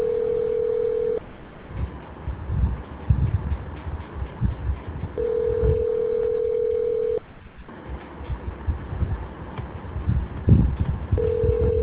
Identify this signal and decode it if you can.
Ringback tone (North America), 440 + 480 Hz.